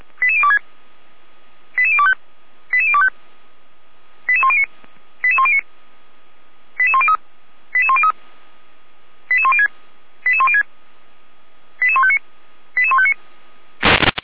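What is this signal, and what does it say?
Five-tone selective call (ZVEI), heard as quick runs of five tones, tones of about 70 ms. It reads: 88026 88025 88025 88109 88109 88103 88103 88106 88106 88159 88159
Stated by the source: ZVEI five-tone selective call received with an SDR, archived by the Feuerwehr Orlamünde fire brigade.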